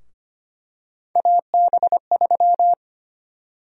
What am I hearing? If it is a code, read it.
Morse code, heard as a keyed tone, about 25 wpm, 700 Hz. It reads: AB3
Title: USA Call Sign Prefixes - Repeated 2x - 25wpm